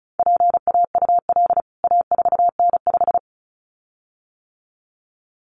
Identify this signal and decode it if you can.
Morse code, heard as a keyed tone, about 34 wpm, 700 Hz. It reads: PAUL A4N5
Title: CWT Exchange 35wpm